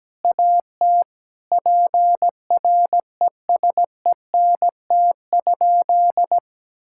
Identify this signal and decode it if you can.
Morse code, heard as a keyed tone, about 17 wpm, 700 Hz. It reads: AT PRESENT?
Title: Sentences from Top 1000 Words - Review Part 6 - 17wpm